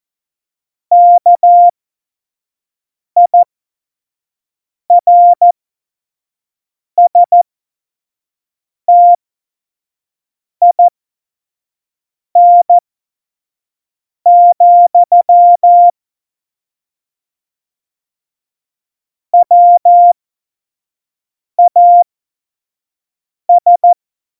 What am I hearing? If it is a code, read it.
Morse code, heard as a keyed tone, about 14 wpm, spaced as for 5 wpm, 700 Hz. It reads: KIRSTIN, WAS
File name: Morsekurs / Witze / Witz 30. Juli 2019, 25 F70